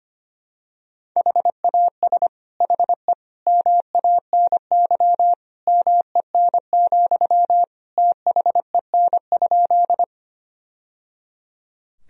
Morse code, heard as a keyed tone, about 25 wpm, 700 Hz. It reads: HAS HE MANY MEN, THEN?
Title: Sentences from Top 1000 Words - Review Part 1 - 25wpm